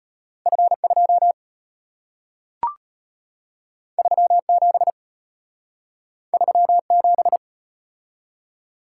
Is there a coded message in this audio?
Yes: Morse code from 0.5 to 7.4 s, heard as a keyed tone, about 34 wpm, 700 Hz. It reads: F2 37 37